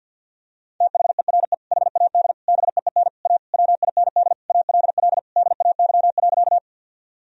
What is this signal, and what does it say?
Morse code, heard as a keyed tone, about 52 wpm, 700 Hz. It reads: THERE HAD BEEN A WIND ALL DAY.
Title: Sentences from Top 1000 Words - Review Part 2 - 50wpm